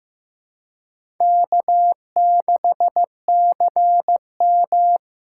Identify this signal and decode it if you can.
Morse code, heard as a keyed tone, about 15 wpm, 700 Hz. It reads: K6CM
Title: USA Call Signs - 4x Character Spacing - 15wpm